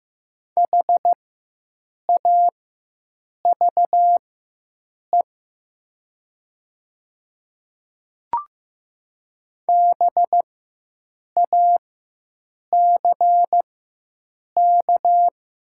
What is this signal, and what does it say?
Morse code, heard as a keyed tone, about 15 wpm, spaced as for 7 wpm, 700 Hz. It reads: HAVE BACK
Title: 4x Character Spacing - Top 200 Words - 15wpm